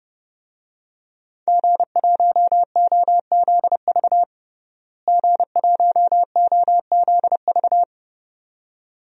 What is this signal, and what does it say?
Morse code, heard as a keyed tone, about 32 wpm, 700 Hz. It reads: G1OZV G1OZV
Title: DXCC-100 Call Signs - 5 Characters - Repeated 2x - 30wpm